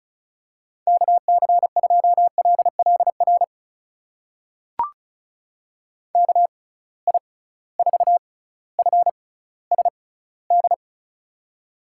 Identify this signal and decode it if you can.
Morse code, heard as a keyed tone, about 34 wpm, spaced as for 18 wpm, 700 Hz. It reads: KC2LLR KI4FSD
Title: USA Call Signs - 6x Character Spacing - 35wpm